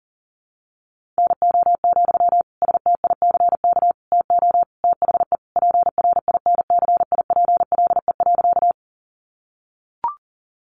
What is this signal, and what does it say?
Morse code, heard as a keyed tone, about 40 wpm, 700 Hz. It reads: NO, STICK TO THE PRINCIPLE.
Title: Sentences from Top 1400 Words 40wpm